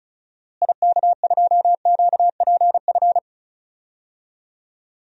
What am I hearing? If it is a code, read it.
Morse code, heard as a keyed tone, about 34 wpm, 700 Hz. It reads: IK2QPF
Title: DXCC-100 Call Signs - 6 Characters - Repeated 2x - 35wpm